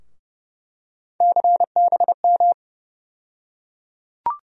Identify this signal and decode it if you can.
Morse code, heard as a keyed tone, about 32 wpm, 700 Hz. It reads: CBM
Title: USA Call Sign Postfixes - 3 Letters 30wpm